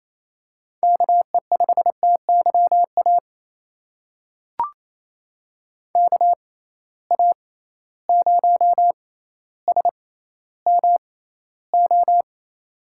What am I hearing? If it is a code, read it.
Morse code, heard as a keyed tone, about 28 wpm, spaced as for 14 wpm, 700 Hz. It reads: KE5TYA KA0SMO
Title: USA Call Signs - 6 Characters - 6x Character Spacing - 28wpm